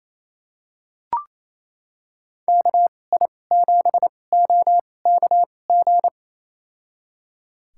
Morse code, heard as a keyed tone, about 28 wpm, spaced as for 20 wpm, 700 Hz. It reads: KI7OKG
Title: USA Call Signs - 6 Characters - 2x Character Spacing - 28wpm